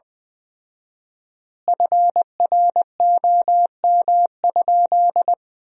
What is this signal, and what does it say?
Morse code, heard as a keyed tone, about 20 wpm, 700 Hz. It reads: FROM?
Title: 4x Word-Spacing - Sentences from Top 300 Words - 20wpm